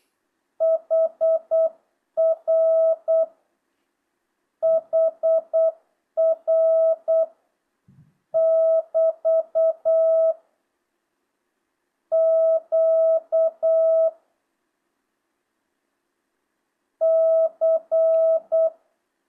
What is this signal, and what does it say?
Morse code, heard as a keyed tone, about 8 wpm, 1300 Hz. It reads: HR HR <BT> Q C
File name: jh8cbh / 2025cw